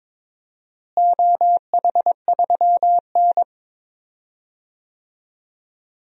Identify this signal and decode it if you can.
Morse code, heard as a keyed tone, about 22 wpm, 700 Hz. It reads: OH3N
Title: DXCC-100 Call Signs - 4 Characters - Repeated 3x - 22wpm